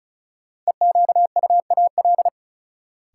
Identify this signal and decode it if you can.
Morse code, heard as a keyed tone, about 36 wpm, 700 Hz. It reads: EQUAL
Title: Sentences from Top 1400 Words - New Word Drill Alt - 35wpm